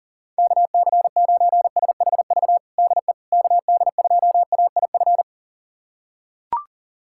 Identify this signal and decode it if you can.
Morse code, heard as a keyed tone, about 40 wpm, 690 Hz. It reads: KC9SHV DE KD2AIF
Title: USA Callsign DE Callsign - 6 Characters - 40wpm